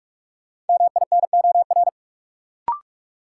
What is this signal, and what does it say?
Morse code, heard as a keyed tone, about 46 wpm, 700 Hz. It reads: MINOR
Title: Sentences from Top 1500 Words - New Word Drill - 45wpm